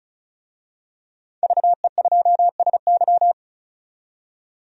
Morse code, heard as a keyed tone, about 36 wpm, 700 Hz. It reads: VE2SY
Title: DXCC-100 Call Signs - 5 Characters - 35wpm